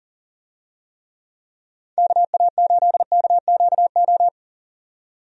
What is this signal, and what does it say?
Morse code, heard as a keyed tone, about 41 wpm, 700 Hz. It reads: KA8KQO
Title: USA Call Signs - 6 Characters - 40wpm